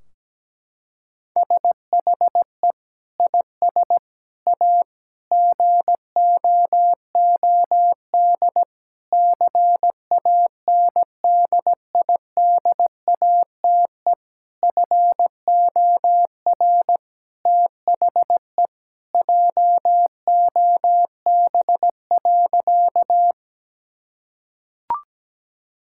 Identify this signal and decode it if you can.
Morse code, heard as a keyed tone, about 17 wpm, 700 Hz. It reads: SHE IS A GOOD CANDIDATE FOR THE JOB.
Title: Sentences from Top 1600 Words 17wpm